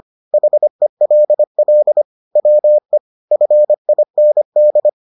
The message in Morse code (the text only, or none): HELL WE FIND